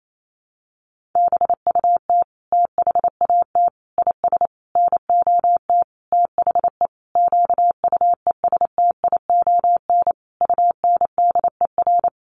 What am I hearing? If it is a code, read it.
Morse code, heard as a keyed tone, about 28 wpm, 700 Hz. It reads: BUT THAT IS NOT THE QUESTION UNDER
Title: Sentences from Top 1500 Words 28wpm